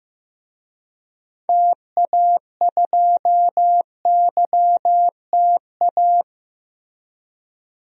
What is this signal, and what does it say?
Morse code, heard as a keyed tone, about 16 wpm, 700 Hz. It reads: TA2YTA
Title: DXCC-100 Call Signs - 6 Characters - 15wpm